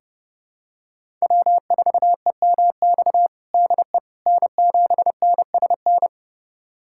Morse code, heard as a keyed tone, about 32 wpm, 700 Hz. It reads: W4EMX DE N7NSN
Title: USA Callsign DE Callsign - 5 Characters - 30wpm